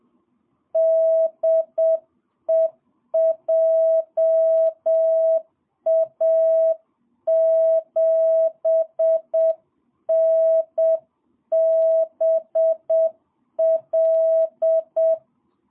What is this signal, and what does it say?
Morse code, heard as a keyed tone, about 7 wpm, 660 Hz. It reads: DE JA7NBL
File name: jh8cbh / 2022cw